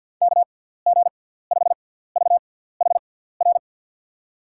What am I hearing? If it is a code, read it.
Morse code, heard as a keyed tone, about 52 wpm, spaced as for 17 wpm, 700 Hz. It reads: KG5VHR
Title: USA Call Signs - 6 Characters - 6x Character Spacing - 50wpm